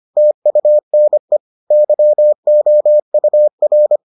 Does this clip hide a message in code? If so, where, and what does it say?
Yes: Morse code from 0.2 to 4.0 s, heard as a keyed tone, about 25 wpm, 600 Hz. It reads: TUNE YOUR